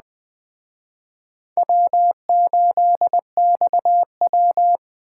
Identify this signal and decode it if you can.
Morse code, heard as a keyed tone, about 21 wpm, 700 Hz. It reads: W8XW